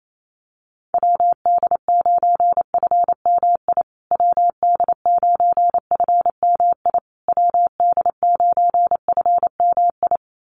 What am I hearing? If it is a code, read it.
Morse code, heard as a keyed tone, about 28 wpm, 700 Hz. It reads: WD9FMI WD9FMI WD9FMI
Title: USA Call Signs Repeated 3x 28wpm